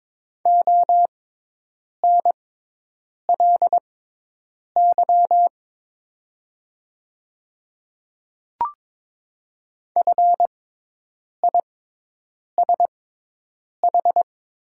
Morse code, heard as a keyed tone, about 22 wpm, spaced as for 8 wpm, 700 Hz. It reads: ONLY FISH
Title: 6x Character Spacing - Top 500 Words - 22wpm